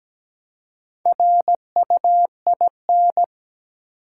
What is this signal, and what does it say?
Morse code, heard as a keyed tone, about 17 wpm, 700 Hz. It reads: RUIN
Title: Four Letter Words - 17wpm